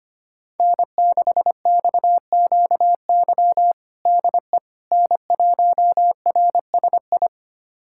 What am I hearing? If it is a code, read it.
Morse code, heard as a keyed tone, about 25 wpm, 700 Hz. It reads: N6XQY DE N1RSI
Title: USA Callsign DE Callsign - 5 Characters - 25wpm